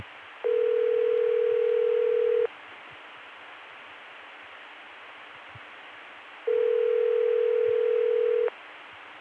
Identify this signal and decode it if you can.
Ringback tone (North America), 440 + 480 Hz.